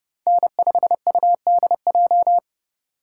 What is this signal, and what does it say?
Morse code, heard as a keyed tone, about 32 wpm, 700 Hz. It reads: N5UDJ
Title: USA Call Signs - 5 Characters - 4x Character Spacing - 30wpm